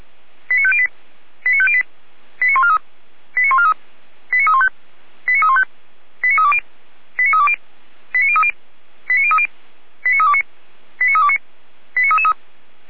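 Five-tone selective call (ZVEI), heard as quick runs of five tones, tones of about 70 ms. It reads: 89508 89508 89143 89143 89315 89315 89320 89320 89030 89030 89329 89329 89333
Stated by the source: ZVEI five-tone selective call received by SDR, from the Feuerwehr Orlamünde fire brigade audio archive.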